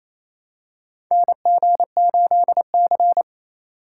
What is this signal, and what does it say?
Morse code, heard as a keyed tone, about 28 wpm, 700 Hz. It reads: NG8C
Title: USA Call Signs - 4 Characters - 4x Character Spacing - 28wpm